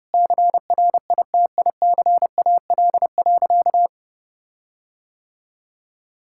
Morse code, heard as a keyed tone, about 30 wpm, 700 Hz. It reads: CRITICAL.
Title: Sentences from Top 1600 Words 30wpm